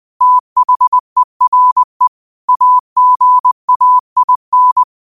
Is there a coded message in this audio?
Yes: Morse code from 0.2 to 4.8 s, heard as a keyed tone, about 20 wpm, 1000 Hz. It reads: THERE AGAIN